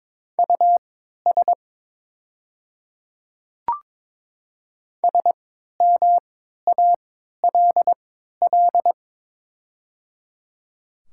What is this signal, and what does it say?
Morse code, heard as a keyed tone, about 22 wpm, spaced as for 12 wpm, 690 Hz. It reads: US SMALL
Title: 3x Character Spacing - Top 200 Words - 22wpm